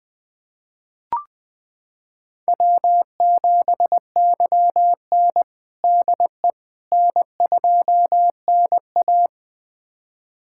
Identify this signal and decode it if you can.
Morse code, heard as a keyed tone, about 20 wpm, 700 Hz. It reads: W7YN DE N2NA
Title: USA Callsign DE Callsign - 4 Characters - 20wpm